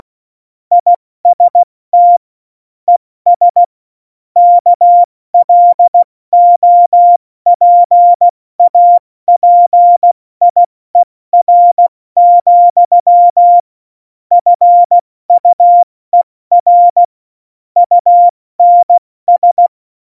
Morse code, heard as a keyed tone, about 16 wpm, 700 Hz. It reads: IST ES KLOPAPIER, FUER UNS
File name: Morsekurs / Witze / Witz 09. April 2019, 70 F80